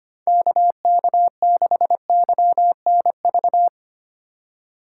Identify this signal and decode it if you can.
Morse code, heard as a keyed tone, about 32 wpm, 700 Hz. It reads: KK6YNV